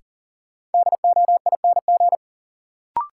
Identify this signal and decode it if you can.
Morse code, heard as a keyed tone, about 40 wpm, 700 Hz. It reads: DOING
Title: Sets of 4 Words - New Word Drill - 40wpm